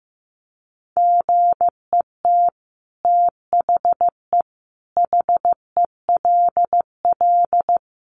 Morse code, heard as a keyed tone, about 15 wpm, 700 Hz. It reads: GET THE HELL
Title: Sets of 5 Words Encore2 15wpm